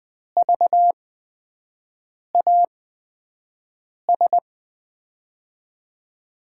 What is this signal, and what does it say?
Morse code, heard as a keyed tone, about 20 wpm, spaced as for 5 wpm, 700 Hz. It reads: VAS
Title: USA Call Signs - 5 Characters - 8x Character Spacing - 20wpm